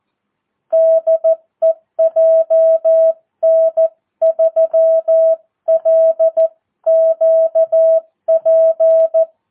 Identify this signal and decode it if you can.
Morse code, heard as a keyed tone, about 14 wpm, 650 Hz. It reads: DE JN3LQP
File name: jh8cbh / 2022cw